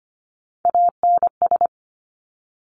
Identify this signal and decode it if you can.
Morse code, heard as a keyed tone, about 25 wpm, 700 Hz. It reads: ANS
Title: Top Three Letter Combinations in Words 25wpm